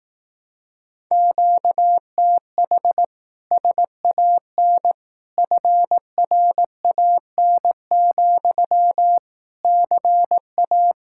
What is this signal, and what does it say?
Morse code, heard as a keyed tone, about 18 wpm, 700 Hz. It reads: QTH SAN FRAN, CA
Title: Speed-Racing Common QSO Phrases Encore - 15wpm